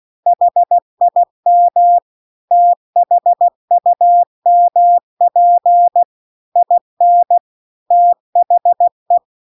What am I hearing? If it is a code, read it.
Morse code, heard as a keyed tone, about 16 wpm, 700 Hz. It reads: HIM THUMP IN THE